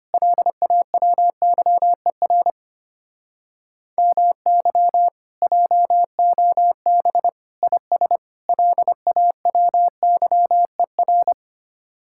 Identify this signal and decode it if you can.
Morse code, heard as a keyed tone, about 32 wpm, 700 Hz. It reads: LAWYER MY JOB IS LAWYER